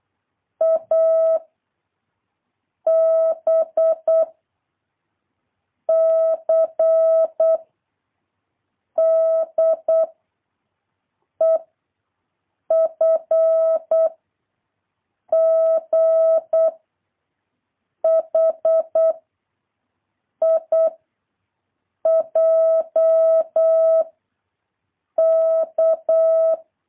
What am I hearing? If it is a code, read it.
Morse code, heard as a keyed tone, about 8 wpm, 650 Hz. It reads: A B C D E F G H I J K
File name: jh8cbh / 2023cw